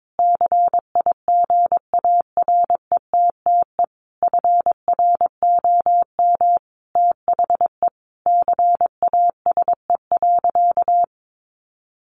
Morse code, heard as a keyed tone, about 22 wpm, 700 Hz. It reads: CIGARETTE FROM THE CASE.
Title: Sentences from Top 1000 Words 22wpm